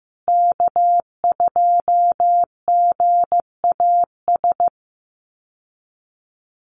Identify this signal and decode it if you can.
Morse code, heard as a keyed tone, about 15 wpm, 700 Hz. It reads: K2GAS
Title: USA Call Signs Repeated 2x 15wpm